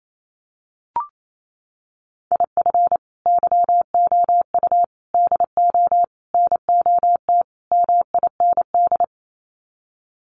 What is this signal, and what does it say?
Morse code, heard as a keyed tone, about 28 wpm, 700 Hz. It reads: IF YOU DO NOT MIND
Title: Sets of 5 Words 28wpm